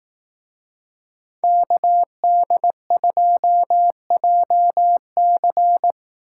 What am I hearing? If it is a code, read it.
Morse code, heard as a keyed tone, about 18 wpm, 700 Hz. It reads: KD2JC
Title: Speed Racing - USA 2x2 Call Signs 15wpm